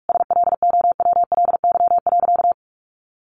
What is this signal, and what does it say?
Morse code, heard as a keyed tone, about 44 wpm, 700 Hz. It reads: SLOWLY.
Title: Sentences from Top 1100 Words 45wpm